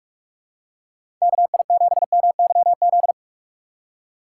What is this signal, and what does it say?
Morse code, heard as a keyed tone, about 44 wpm, 700 Hz. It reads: KI7MYZ
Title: USA Call Signs - 6 Characters - 4x Character Spacing - 45wpm